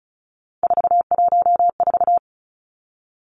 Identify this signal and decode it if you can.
Morse code, heard as a keyed tone, about 34 wpm, 700 Hz. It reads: 414
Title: Numbers - 3-Digit 35wpm